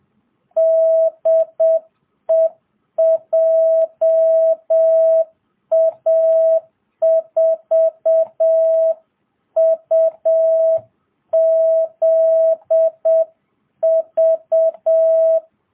Morse code, heard as a keyed tone, about 7 wpm, 660 Hz. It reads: DE JA4UZV